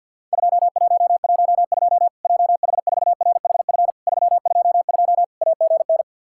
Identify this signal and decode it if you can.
Morse code, heard as a keyed tone, about 50 wpm, 700 Hz. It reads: 2112 JH4RHF 322 AGN
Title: Worked All Europe DX Contest - QTC Exchange with Repeats - 50wpm